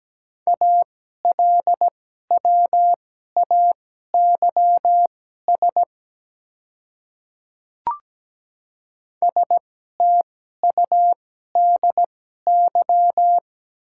Morse code, heard as a keyed tone, about 17 wpm, spaced as for 12 wpm, 700 Hz. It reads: ALWAYS STUDY
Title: 2x Character Spacing - Top 300 Words - 17wpm